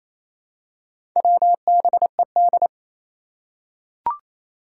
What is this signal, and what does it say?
Morse code, heard as a keyed tone, about 29 wpm, 700 Hz. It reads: WBED
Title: Four Challenging Letter-Number Combinations 28wpm